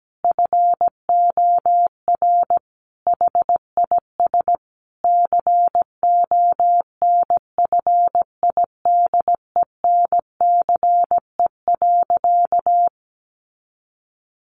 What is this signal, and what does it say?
Morse code, heard as a keyed tone, about 17 wpm, 700 Hz. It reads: FOR HIS CONFIDENCE.
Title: Sentences from Top 1500 Words 17wpm